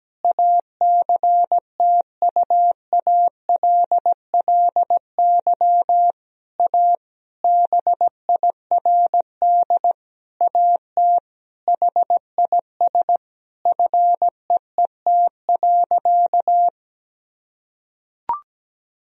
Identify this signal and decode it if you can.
Morse code, heard as a keyed tone, about 17 wpm, 700 Hz. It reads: ACTUALLY A BIRD AT HIS FEET.
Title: Sentences from Top 1000 Words - Review Part 4 - 17wpm